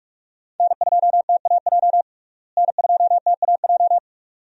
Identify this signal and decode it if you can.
Morse code, heard as a keyed tone, about 46 wpm, 700 Hz. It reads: N2TAJ N2TAJ
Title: USA Call Signs - 5 Characters - Repeated 2x - 45wpm